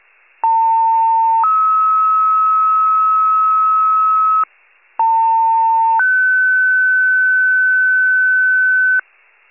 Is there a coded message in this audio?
Yes: two-tone page from 0.4 to 9.0 s, heard as a one-second tone, then a three-second tone, about 900 then 1290 Hz, about 900 then 1500 Hz, with noise.